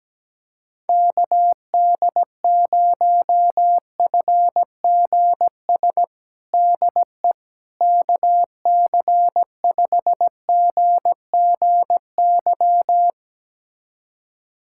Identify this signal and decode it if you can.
Morse code, heard as a keyed tone, about 17 wpm, 700 Hz. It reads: KD0FGS DE KC5GGY
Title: USA Callsign DE Callsign - 6 Characters - 17wpm